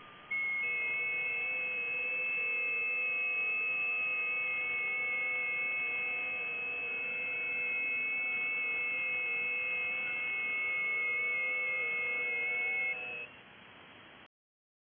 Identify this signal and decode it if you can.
FT8.